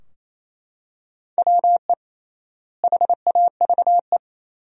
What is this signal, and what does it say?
Morse code, heard as a keyed tone, about 28 wpm, 700 Hz. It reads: WE HAVE
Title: 3x Word-Spacing - Sentences from Top 100 Words - 28wpm